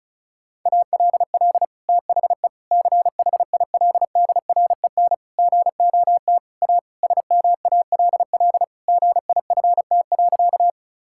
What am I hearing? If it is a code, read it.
Morse code, heard as a keyed tone, about 36 wpm, 700 Hz. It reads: ALL THE CHILDREN GOT A SMALL GIFT.
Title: Sentences from Top 1000 Words - Review Part 6 - 35wpm